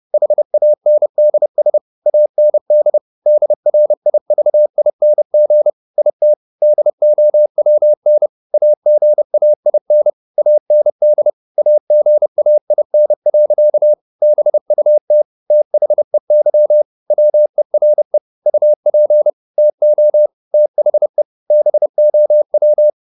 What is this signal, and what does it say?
Morse code, heard as a keyed tone, about 30 wpm, 600 Hz. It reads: HANDS AND DRIVING IT DOWN AGAIN AND AGAIN. BUT THEY WERE UP TO THE BOW